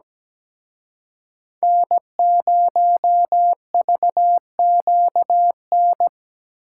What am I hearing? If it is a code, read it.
Morse code, heard as a keyed tone, about 20 wpm, 700 Hz. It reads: N0VQN
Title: Speed Racing - USA 1x3 Call Signs 17wpm